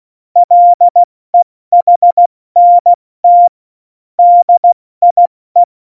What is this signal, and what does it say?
Morse code, heard as a keyed tone, about 16 wpm, 700 Hz. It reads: LEHNT DIE
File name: Morsekurs / Witze / Witz 27. Oktober 2020, 70 F80